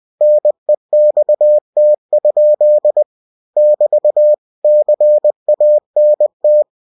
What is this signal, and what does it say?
Morse code, heard as a keyed tone, about 20 wpm, 600 Hz. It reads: NEXT? <BT> CANT
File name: cw-books / winnie-the-pooh / winnie-20wpm-page-0072